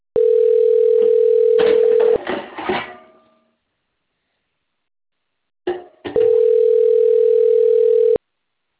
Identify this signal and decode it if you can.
Ringback tone (North America), 440 + 480 Hz.